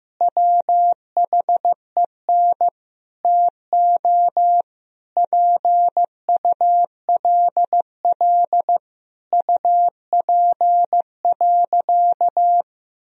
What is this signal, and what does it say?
Morse code, heard as a keyed tone, about 15 wpm, 700 Hz. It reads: WHEN TO PULL UP.
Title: Sentences from Top 1000 Words - Review Part 2 - 15wpm